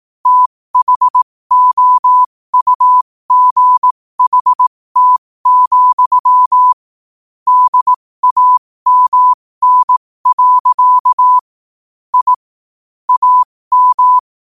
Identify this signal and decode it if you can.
Morse code, heard as a keyed tone, about 18 wpm, spaced as for 15 wpm, 1000 Hz. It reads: THOUGHT, DAMN. I AM
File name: qotdfiles / 15wpm